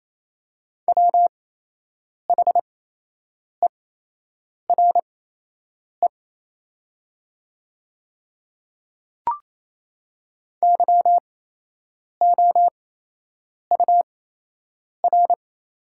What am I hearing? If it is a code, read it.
Morse code, heard as a keyed tone, about 28 wpm, spaced as for 8 wpm, 700 Hz. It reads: WHERE YOUR
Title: 8x Character Spacing - Top 500 Words - 28wpm